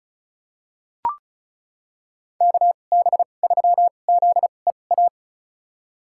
Morse code, heard as a keyed tone, about 36 wpm, spaced as for 25 wpm, 700 Hz. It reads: KB3ZEA
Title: USA Call Signs - 6 Characters - 2x Character Spacing - 35wpm